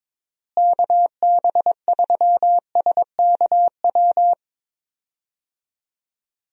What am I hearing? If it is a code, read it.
Morse code, heard as a keyed tone, about 22 wpm, 700 Hz. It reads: KB3SKW
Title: USA Call Signs - 6 Characters - Repeated 3x - 22wpm